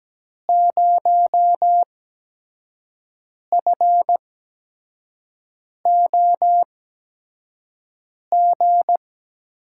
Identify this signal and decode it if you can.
Morse code, heard as a keyed tone, about 17 wpm, spaced as for 5 wpm, 700 Hz. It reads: 0FOG
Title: USA Call Signs - 5 Characters - 8x Character Spacing - 17wpm